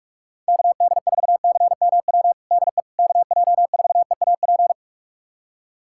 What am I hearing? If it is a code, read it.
Morse code, heard as a keyed tone, about 42 wpm, 690 Hz. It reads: KD4CMW DE KJ4EAP
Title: USA Callsign DE Callsign 45wpm